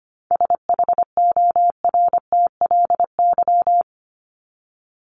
Morse code, heard as a keyed tone, about 25 wpm, 700 Hz. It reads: SHORTLY
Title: LY Suffix - Top 100 Words with Base Words 25wpm